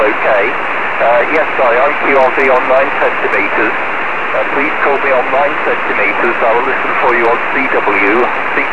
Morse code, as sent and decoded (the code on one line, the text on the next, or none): none